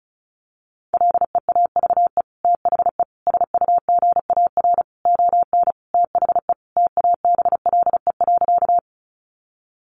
Morse code, heard as a keyed tone, about 34 wpm, 700 Hz. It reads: LEAVE THE SUGAR ON THE TABLE.
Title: Sentences from Top 1200 Words 35wpm